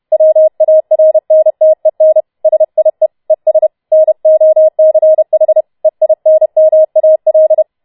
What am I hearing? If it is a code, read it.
Morse code, heard as a keyed tone, about 31 wpm, 630 Hz. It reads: WARNTEN SIE ES NOCH EINMAL